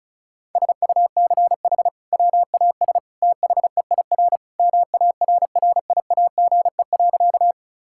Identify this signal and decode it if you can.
Morse code, heard as a keyed tone, about 36 wpm, 700 Hz. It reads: SUCH WAS THEIR MARRIAGE.
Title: Sentences from Top 1000 Words - Review Part 5 - 35wpm